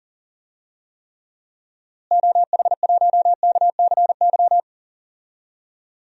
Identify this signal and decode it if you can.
Morse code, heard as a keyed tone, about 40 wpm, 700 Hz. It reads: OH1KCY
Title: DXCC-100 Call Signs - 6 Characters - Repeated 3x - 40wpm